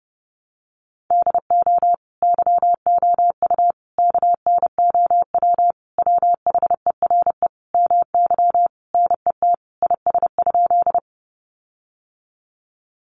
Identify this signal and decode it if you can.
Morse code, heard as a keyed tone, about 28 wpm, 700 Hz. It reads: DO YOU KNOW WHERE MY NET IS?
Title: Sentences from Top 1400 Words 30wpm